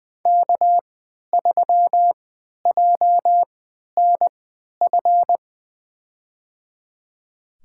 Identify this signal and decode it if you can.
Morse code, heard as a keyed tone, about 21 wpm, spaced as for 12 wpm, 700 Hz. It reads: K3JNF
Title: USA Call Signs - 3x Character Spacing - 20wpm